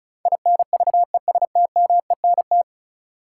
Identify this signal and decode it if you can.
Morse code, heard as a keyed tone, about 36 wpm, 700 Hz. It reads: INVESTMENT
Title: Sentences from Top 700 Words - New Word Drill - 35wpm